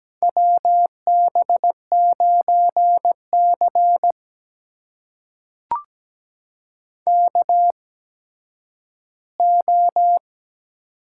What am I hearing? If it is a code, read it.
Morse code, heard as a keyed tone, about 17 wpm, spaced as for 10 wpm, 700 Hz. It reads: WB9C KO
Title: USA Call Signs - 4 Characters - 8x Character Spacing - 17wpm